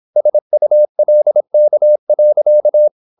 Morse code, heard as a keyed tone, about 26 wpm, 600 Hz. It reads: SULK.